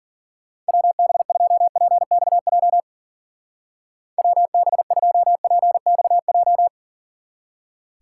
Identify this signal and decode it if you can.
Morse code, heard as a keyed tone, about 49 wpm, 700 Hz. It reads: WB2PXJ WB2PXJ
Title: Speed Racing - USA 2x3 Call Signs 40wpm